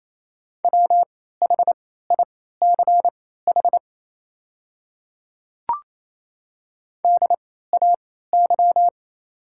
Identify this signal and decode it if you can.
Morse code, heard as a keyed tone, about 28 wpm, spaced as for 16 wpm, 690 Hz. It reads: WHICH DAY
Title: 3x Character Spacing - Top 100 Words - 28wpm